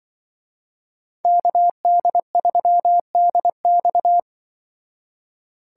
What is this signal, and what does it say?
Morse code, heard as a keyed tone, about 24 wpm, 700 Hz. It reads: KD3DX
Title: Speed Racing - USA 2x2 Call Signs 20wpm